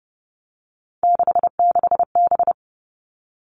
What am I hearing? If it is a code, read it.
Morse code, heard as a keyed tone, about 30 wpm, 700 Hz. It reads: 66B
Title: All Cut Numbers - 3-Digit 30wpm